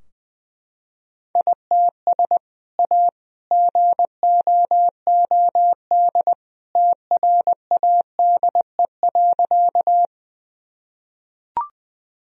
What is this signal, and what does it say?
Morse code, heard as a keyed tone, about 20 wpm, 700 Hz. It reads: ITS A GOOD TRADE.
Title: Sentences from Top 1000 Words - Review Part 4 - 20wpm